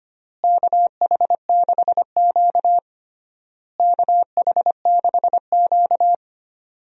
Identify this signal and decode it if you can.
Morse code, heard as a keyed tone, about 25 wpm, 700 Hz. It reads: KH6Q KH6Q
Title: DXCC-100 Call Signs - 4 Characters - Repeated 3x - 25wpm